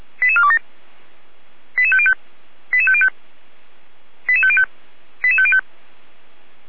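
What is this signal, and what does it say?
Five-tone selective call (ZVEI), heard as quick runs of five tones, tones of about 70 ms. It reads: 88427 88505 88505 88505 88505
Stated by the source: ZVEI five-tone selective call received on SDR.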